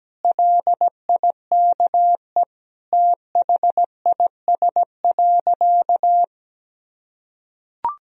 Morse code, heard as a keyed tone, about 17 wpm, 700 Hz. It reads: LIKE THIS.
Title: Sentences from Top 1000 Words - Review Part 2 - 17wpm